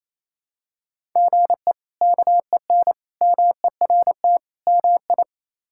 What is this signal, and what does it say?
Morse code, heard as a keyed tone, about 28 wpm, 700 Hz. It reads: GE KEN MERT MI
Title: K1USN SST QSO Practice - 28wpm